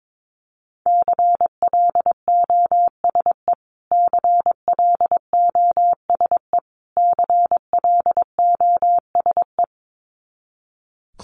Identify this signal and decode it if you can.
Morse code, heard as a keyed tone, about 22 wpm, 700 Hz. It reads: CLOSE CLOSE CLOSE
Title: Top 500 Words Repeated 22wpm